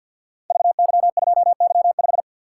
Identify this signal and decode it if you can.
Morse code, heard as a keyed tone, about 46 wpm, 700 Hz. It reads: VY2Y5
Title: Code Groups - 5 Letter:Numbers - 50wpm